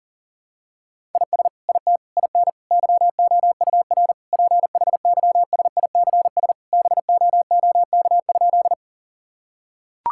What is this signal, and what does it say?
Morse code, heard as a keyed tone, about 40 wpm, 700 Hz. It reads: IS IT IN YOUR PHYSICS BOOK?
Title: Sentences from Top 1000 Words - Review Part 5 - 40wpm